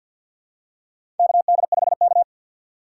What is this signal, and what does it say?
Morse code, heard as a keyed tone, about 46 wpm, 700 Hz. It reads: KD5K
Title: USA Call Signs - 2x Character Spacing - 50wpm